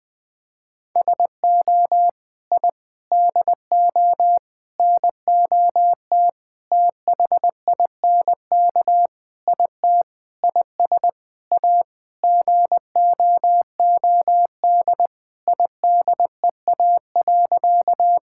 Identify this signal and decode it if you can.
Morse code, heard as a keyed tone, about 20 wpm, 700 Hz. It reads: SO I DO NOT THINK IT IS A GOOD IDEA.
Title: Sentences with 5-Word Phrases - Spoken Beginning - 20wpm